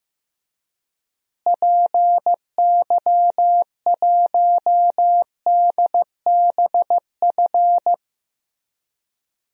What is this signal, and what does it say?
Morse code, heard as a keyed tone, about 16 wpm, 700 Hz. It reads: PY1DBF
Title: DXCC-100 Call Signs - 6 Characters - Repeated 2x - 15wpm